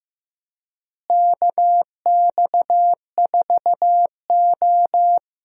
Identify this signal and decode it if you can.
Morse code, heard as a keyed tone, about 15 wpm, 700 Hz. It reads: KX4O
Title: USA Call Signs - 4 Characters - Repeated 2x - 15wpm